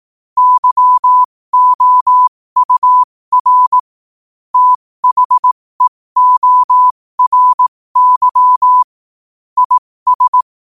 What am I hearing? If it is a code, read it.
Morse code, heard as a keyed tone, about 18 wpm, spaced as for 15 wpm, 1000 Hz. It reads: YOUR THEORY IS
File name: qotdfiles / 15wpm